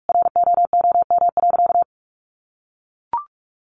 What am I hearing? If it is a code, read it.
Morse code, heard as a keyed tone, about 44 wpm, 700 Hz. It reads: ROOM.
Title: Sentences from Top 1100 Words 45wpm